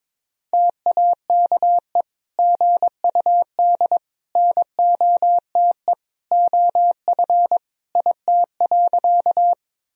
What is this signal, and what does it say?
Morse code, heard as a keyed tone, about 22 wpm, 700 Hz. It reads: TAKE GUD NOTE OF IT.
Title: Sentences from Top 500 Words and Abbreviations 22wpm